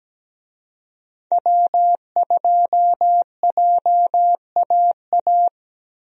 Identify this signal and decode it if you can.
Morse code, heard as a keyed tone, about 17 wpm, 700 Hz. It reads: W2JAA
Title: USA Call Signs - 5 Characters - Repeated 2x - 17wpm